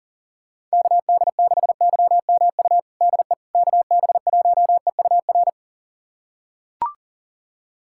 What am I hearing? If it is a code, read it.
Morse code, heard as a keyed tone, about 40 wpm, 700 Hz. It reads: KD6YMU DE KB1EUR
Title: USA Callsign DE Callsign - 6 Characters - 40wpm